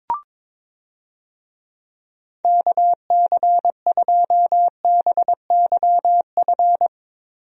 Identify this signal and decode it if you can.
Morse code, heard as a keyed tone, about 22 wpm, 700 Hz. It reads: KC2BYF
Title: USA Call Signs - 6 Characters - Repeated 3x - 22wpm